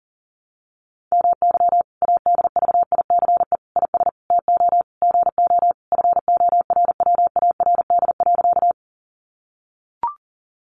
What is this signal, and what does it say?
Morse code, heard as a keyed tone, about 40 wpm, 700 Hz. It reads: MY ADVICE IS TO GO FORWARD.
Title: Sentences from Top 1300 Words 40wpm